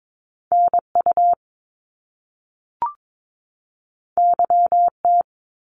Morse code, heard as a keyed tone, about 22 wpm, 700 Hz. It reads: NU YT
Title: Canadian Provinces and Territories 22wpm